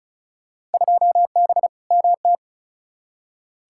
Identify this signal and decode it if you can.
Morse code, heard as a keyed tone, about 34 wpm, 700 Hz. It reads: 2B MT
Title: Field Day Exchanges Encore - 35wpm